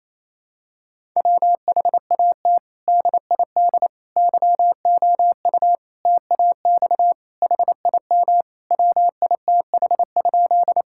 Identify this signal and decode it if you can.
Morse code, heard as a keyed tone, about 28 wpm, 700 Hz. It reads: WHAT DID YOU TAX HIM WITH?
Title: Sentences from Top 1000 Words - Review Part 5 - 28wpm